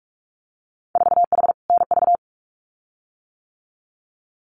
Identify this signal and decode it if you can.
Morse code, heard as a keyed tone, about 46 wpm, 700 Hz. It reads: 4H NV
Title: Winter Field Day Exchanges Encore 45wpm